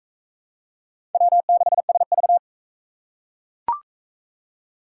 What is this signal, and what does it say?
Morse code, heard as a keyed tone, about 38 wpm, 700 Hz. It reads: W6SV